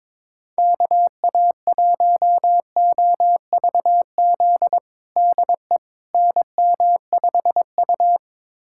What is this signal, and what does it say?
Morse code, heard as a keyed tone, about 21 wpm, 700 Hz. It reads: KA1OVZ DE NM5U
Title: USA Callsign DE Callsign 22wpm